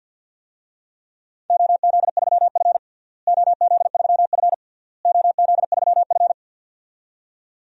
Morse code, heard as a keyed tone, about 52 wpm, 700 Hz. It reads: OZ3F OZ3F OZ3F
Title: DXCC-100 Call Signs - 4 Characters - Repeated 3x - 50wpm